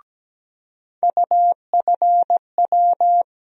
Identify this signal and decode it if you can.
Morse code, heard as a keyed tone, about 17 wpm, 700 Hz. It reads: UFW
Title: USA Call Sign Postfixes - 3 Letters 17wpm